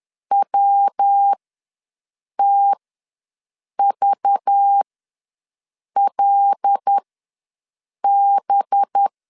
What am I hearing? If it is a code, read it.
Morse code, heard as a keyed tone, about 10 wpm, spaced as for 6 wpm, 800 Hz. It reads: WTVLB
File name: LEZIONI / 5 LBVC / 5 LBVC ws 10wpm 3733